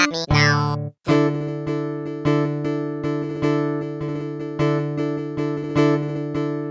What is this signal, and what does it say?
There is no code.